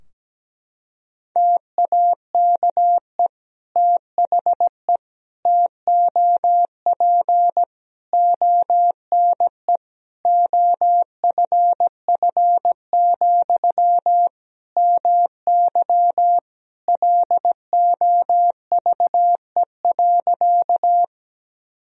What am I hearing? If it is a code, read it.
Morse code, heard as a keyed tone, about 17 wpm, 700 Hz. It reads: TAKE THE TOP ONE OFF, MY LOVE.
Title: Sentences from Top 1000 Words - Review Part 2 - 17wpm